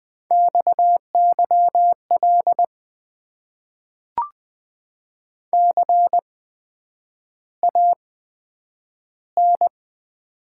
Morse code, heard as a keyed tone, about 20 wpm, spaced as for 9 wpm, 700 Hz. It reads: XYL CAN
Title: Top 90 QSO Elements - 8x Character Spacing - 20wpm